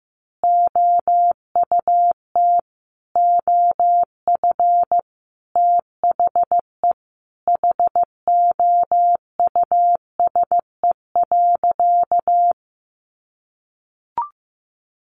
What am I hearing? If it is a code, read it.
Morse code, heard as a keyed tone, about 14 wpm, 700 Hz. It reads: OUT OF THE HOUSE.
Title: Sentences from Top 1300 Words 15wpm